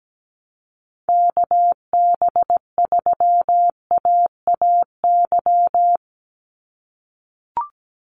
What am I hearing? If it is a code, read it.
Morse code, heard as a keyed tone, about 17 wpm, 700 Hz. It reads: KB3AAY
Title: Call Signs - 17wpm